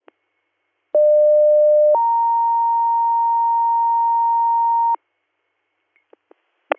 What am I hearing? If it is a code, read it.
Two-tone page, heard as a one-second tone, then a three-second tone, about 600 then 930 Hz. It is heard from 0.9 to 5.0 s.